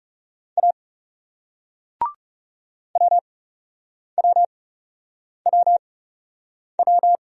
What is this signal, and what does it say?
Morse code, heard as a keyed tone, about 40 wpm, spaced as for 8 wpm, 700 Hz. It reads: A WWWW